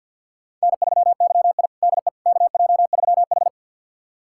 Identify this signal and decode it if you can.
Morse code, heard as a keyed tone, about 50 wpm, 690 Hz. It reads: N3YI DE KJ3H
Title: USA Callsign DE Callsign - 4 Characters - 50wpm